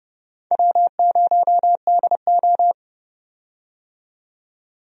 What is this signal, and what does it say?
Morse code, heard as a keyed tone, about 32 wpm, 700 Hz. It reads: W0DO
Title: USA Call Signs - 4 Characters - 30wpm